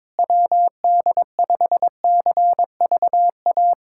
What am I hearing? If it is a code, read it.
Morse code, heard as a keyed tone, about 22 wpm, 700 Hz. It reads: WD5CVA
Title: USA Callsign DE Callsign - 6 Characters - 22wpm